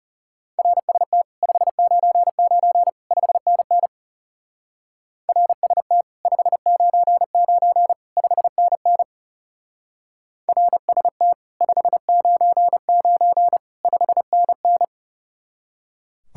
Morse code, heard as a keyed tone, about 34 wpm, 700 Hz. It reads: RST 599 5NN RST 599 5NN RST 599 5NN
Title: Speed-Racing Common QSO Phrases - 30wpm